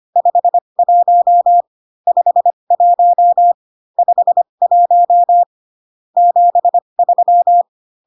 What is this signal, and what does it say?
Morse code, heard as a keyed tone, about 25 wpm, spaced as for 22 wpm, 700 Hz. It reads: 51 51 51 73
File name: CW common abbreviations alphabetical repeated 3 times 25 18wpm